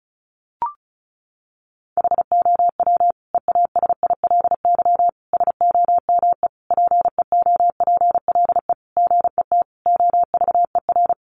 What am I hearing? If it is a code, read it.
Morse code, heard as a keyed tone, about 34 wpm, 700 Hz. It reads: HOW EASILY SOME PEOPLE GET OVER
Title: Sentences from Top 900 Words 35wpm